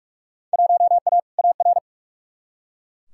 Morse code, heard as a keyed tone, about 44 wpm, 700 Hz. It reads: 1A AR
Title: Field Day Exchanges - 45wpm